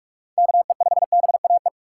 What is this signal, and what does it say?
Morse code, heard as a keyed tone, about 46 wpm, 700 Hz. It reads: KE5BAE
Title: USA Call Signs - 6 Characters - 2x Character Spacing - 45wpm